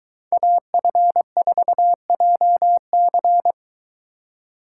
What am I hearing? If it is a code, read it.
Morse code, heard as a keyed tone, about 23 wpm, 700 Hz. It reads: AF4JC